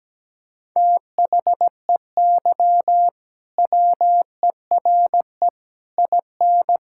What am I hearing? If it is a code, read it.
Morse code, heard as a keyed tone, about 17 wpm, 700 Hz. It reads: THEY WERE IN
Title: Sentences from Top 1000 Words - Review Part 6 - 17wpm